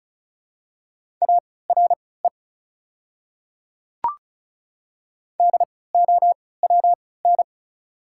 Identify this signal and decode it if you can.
Morse code, heard as a keyed tone, about 34 wpm, spaced as for 20 wpm, 690 Hz. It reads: ARE DOWN
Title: 3x Character Spacing - Top 100 Words - 35wpm